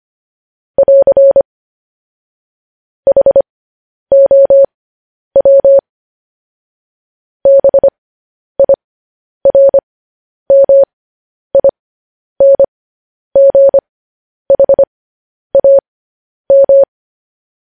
Morse code, heard as a keyed tone, about 25 wpm, spaced as for 10 wpm, 550 Hz. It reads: <AR> HOW BIRMINGHAM
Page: news headlines in morse code at 10 wpm